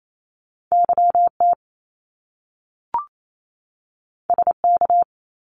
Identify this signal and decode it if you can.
Morse code, heard as a keyed tone, about 28 wpm, 700 Hz. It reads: YT SK
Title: Canadian Provinces and Territories 28wpm